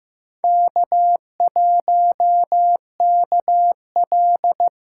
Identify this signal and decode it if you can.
Morse code, heard as a keyed tone, about 15 wpm, 700 Hz. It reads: K1KL
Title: USA Callsign DE Callsign - 4 Characters - 15wpm